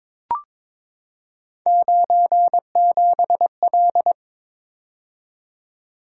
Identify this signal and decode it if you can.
Morse code, heard as a keyed tone, about 22 wpm, 700 Hz. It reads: 97L
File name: practice / 22wpm / Three Letter-Number Combinations - 22wpm